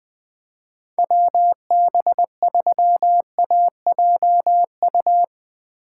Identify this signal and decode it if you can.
Morse code, heard as a keyed tone, about 22 wpm, 700 Hz. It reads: WB3AJU
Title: Speed Racing - USA 2x3 Call Signs 20wpm